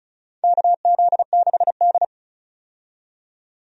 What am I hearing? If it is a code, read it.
Morse code, heard as a keyed tone, about 36 wpm, 700 Hz. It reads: KZ6D
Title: USA Call Signs - 4 Characters - 35wpm